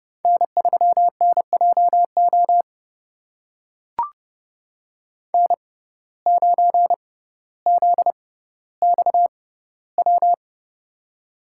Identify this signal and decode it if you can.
Morse code, heard as a keyed tone, about 30 wpm, spaced as for 15 wpm, 700 Hz. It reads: N3NJO N9ZXW
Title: USA Call Signs - 5 Characters - 6x Character Spacing - 30wpm